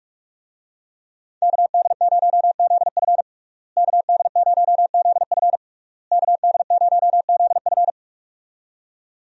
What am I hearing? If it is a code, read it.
Morse code, heard as a keyed tone, about 41 wpm, 700 Hz. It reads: KD0ZF KD0ZF KD0ZF
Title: USA Call Signs - 5 Characters - Repeated 3x - 45wpm